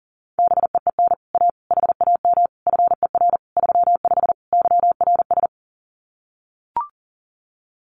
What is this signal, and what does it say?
Morse code, heard as a keyed tone, about 40 wpm, 700 Hz. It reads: BEEN A HAM FER 35 YRS
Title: Common QSO Phrases Encore 40wpm